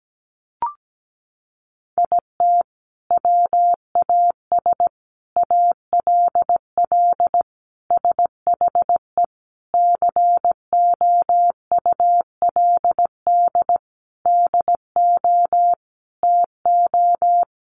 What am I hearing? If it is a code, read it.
Morse code, heard as a keyed tone, about 17 wpm, 700 Hz. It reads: IT WAS ALL SHE COULD DO TO
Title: Sentences from Top 400 Words 17wpm